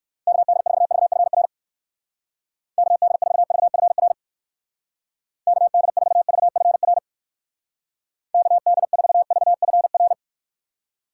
Morse code, heard as a keyed tone, about 55 wpm, 700 Hz. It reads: KD4VFR KD4VFR KD4VFR KD4VFR
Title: Speed Racing - USA 2x3 Call Signs 45wpm